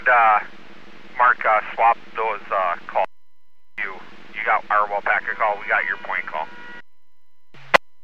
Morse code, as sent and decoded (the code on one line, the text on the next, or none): none